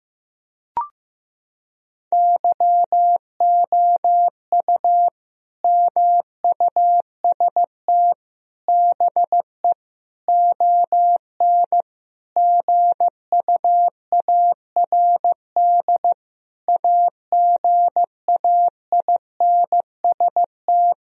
Morse code, heard as a keyed tone, about 15 wpm, 700 Hz. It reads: YOU MUST BE ON GUARD AGAINST
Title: Sentences from Top 1000 Words - Review Part 4 - 15wpm